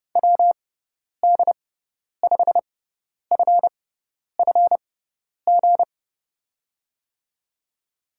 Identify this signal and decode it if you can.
Morse code, heard as a keyed tone, about 30 wpm, spaced as for 10 wpm, 700 Hz. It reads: WD5FFG
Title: USA Call Signs - 6 Characters - 6x Character Spacing - 30wpm